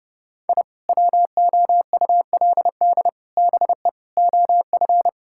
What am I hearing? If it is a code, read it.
Morse code, heard as a keyed tone, about 30 wpm, 700 Hz. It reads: I WOULD BE OF
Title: Sentences from Top 1000 Words - Review Part 5 - 30wpm